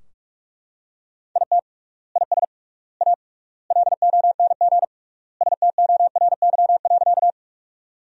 Morse code, heard as a keyed tone, about 42 wpm, 700 Hz. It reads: IT IS A LONG STORY.
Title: 3x Word-Spacing - Sentences from Top 300 Words - 45wpm